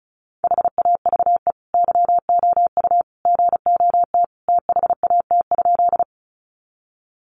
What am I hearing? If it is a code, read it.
Morse code, heard as a keyed tone, about 34 wpm, 700 Hz. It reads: HAVE YOU GOT THAT?
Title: Sentences from Top 300 Words 35wpm